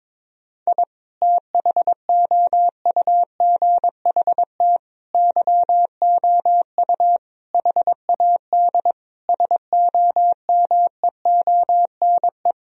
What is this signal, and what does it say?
Morse code, heard as a keyed tone, about 22 wpm, 700 Hz. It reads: I THOUGHT YOU HAD SOMEONE
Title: Sentences from Top 1000 Words - Review Part 4 - 22wpm